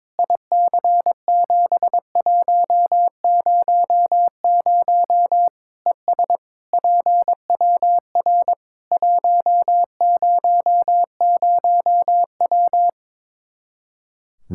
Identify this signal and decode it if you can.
Morse code, heard as a keyed tone, about 22 wpm, 700 Hz. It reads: IC7100 ES PWR 100W
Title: K1USN SST QSO Practice 22wpm